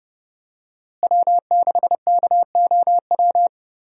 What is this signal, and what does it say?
Morse code, heard as a keyed tone, about 30 wpm, 700 Hz. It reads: W6KOW
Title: USA Call Signs - 5 Characters - Repeated 2x - 30wpm